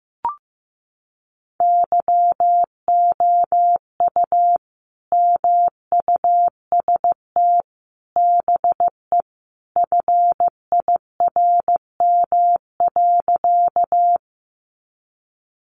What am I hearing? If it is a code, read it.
Morse code, heard as a keyed tone, about 15 wpm, 700 Hz. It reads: YOU MUST BE FIRM.
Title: Sentences from Top 1400 Words 15wpm